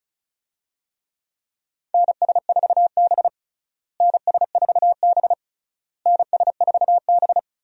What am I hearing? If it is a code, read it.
Morse code, heard as a keyed tone, about 36 wpm, 700 Hz. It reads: NS4B NS4B NS4B
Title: USA Call Signs - 4 Characters - Repeated 3x - 35wpm